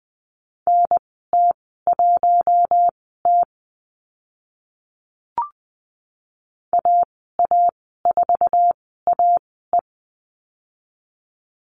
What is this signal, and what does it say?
Morse code, heard as a keyed tone, about 20 wpm, spaced as for 14 wpm, 700 Hz. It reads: NT1T AA4AE
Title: 048 Lesson 9g - Number 4 - Call Signs 20wpm